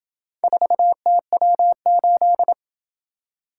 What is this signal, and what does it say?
Morse code, heard as a keyed tone, about 26 wpm, 700 Hz. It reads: 4TW8